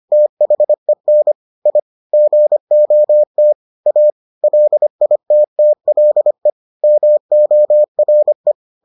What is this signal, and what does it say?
Morse code, heard as a keyed tone, about 25 wpm, 600 Hz. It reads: THEN I GOT A LITTLE MORE